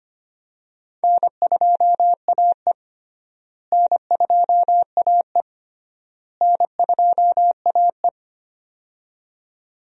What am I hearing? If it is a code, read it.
Morse code, heard as a keyed tone, about 25 wpm, 700 Hz. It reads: N2AE N2AE N2AE
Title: USA Call Signs - 4 Characters - Repeated 3x - 25wpm